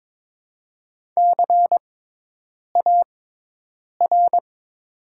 Morse code, heard as a keyed tone, about 22 wpm, spaced as for 8 wpm, 700 Hz. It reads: CAR
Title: 6x Character Spacing - Top 500 Words - 22wpm